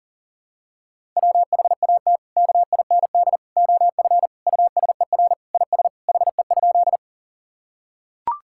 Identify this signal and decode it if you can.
Morse code, heard as a keyed tone, about 40 wpm, 700 Hz. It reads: WHAT KIND OF USER IS HE?
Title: Sentences from Top 1000 Words - Review Part 5 - 40wpm